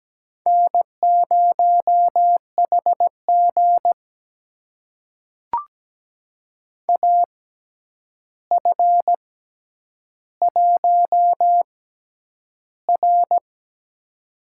Morse code, heard as a keyed tone, about 17 wpm, spaced as for 9 wpm, 700 Hz. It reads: N0HG AF1R
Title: USA Call Signs - 4 Characters - 6x Character Spacing - 17wpm